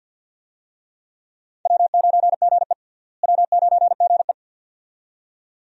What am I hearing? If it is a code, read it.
Morse code, heard as a keyed tone, about 50 wpm, 700 Hz. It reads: W9GE W9GE
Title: USA Call Signs - 4 Characters - Repeated 2x - 50wpm